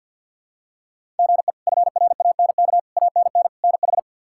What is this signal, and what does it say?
Morse code, heard as a keyed tone, about 47 wpm, 700 Hz. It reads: GE FRANK ANN NH
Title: K1USN SST QSO Practice - 50wpm